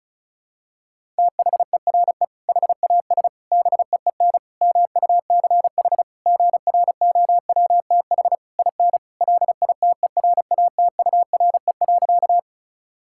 Morse code, heard as a keyed tone, about 36 wpm, 700 Hz. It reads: THERE HAS BEEN MUCH GROWTH IN LITERATURE.
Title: Sentences from Top 1000 Words - Review Part 5 - 35wpm